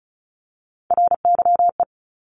Morse code, heard as a keyed tone, about 36 wpm, 700 Hz. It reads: RYE
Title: Multi-letter Phonemes Repeated 3x 35wpm